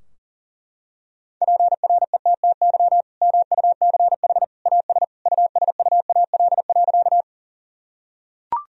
Morse code, heard as a keyed tone, about 40 wpm, 700 Hz. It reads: PRETTY MUCH AS USUAL.
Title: Sentences from Top 1000 Words - Review Part 2 - 40wpm